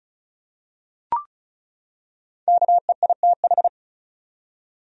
Morse code, heard as a keyed tone, about 34 wpm, 700 Hz. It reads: KEITH
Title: USA Common Names - 35wpm